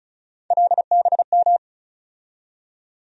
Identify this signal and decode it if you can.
Morse code, heard as a keyed tone, about 36 wpm, 700 Hz. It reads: LBM
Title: USA Call Sign Postfixes - 3 Letters 35wpm